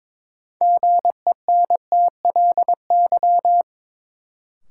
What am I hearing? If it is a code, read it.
Morse code, heard as a keyed tone, about 21 wpm, 700 Hz. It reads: GENTLY
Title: Sentences from Top 1500 Words - New Word Drill Alt - 22wpm